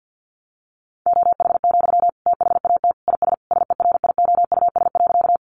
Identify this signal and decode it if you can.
Morse code, heard as a keyed tone, about 50 wpm, 700 Hz. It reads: OH, THAT IS SERIOUS.
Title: Sentences from Top 1100 Words 50wpm